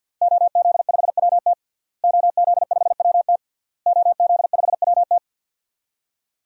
Morse code, heard as a keyed tone, about 52 wpm, 700 Hz. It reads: OZ5WT OZ5WT OZ5WT
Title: DXCC-100 Call Signs - 5 Characters - Repeated 3x - 50wpm